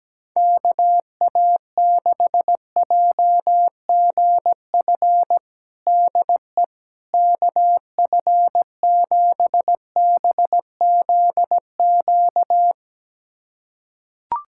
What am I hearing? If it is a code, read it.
Morse code, heard as a keyed tone, about 17 wpm, 700 Hz. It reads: KA6JGF DE KF7BZQ
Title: USA Callsign DE Callsign - 6 Characters - 17wpm